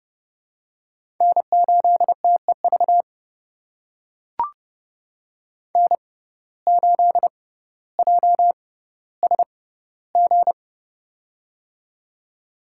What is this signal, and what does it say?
Morse code, heard as a keyed tone, about 30 wpm, spaced as for 15 wpm, 700 Hz. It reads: N8TEV N8JSG
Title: USA Call Signs - 5 Characters - 6x Character Spacing - 30wpm